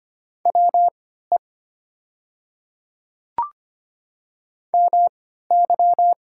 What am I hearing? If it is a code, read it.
Morse code, heard as a keyed tone, about 25 wpm, spaced as for 14 wpm, 700 Hz. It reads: WE MY
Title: 3x Character Spacing - Top 100 Words - 25wpm